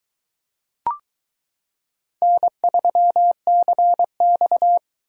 Morse code, heard as a keyed tone, about 22 wpm, 700 Hz. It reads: N3CX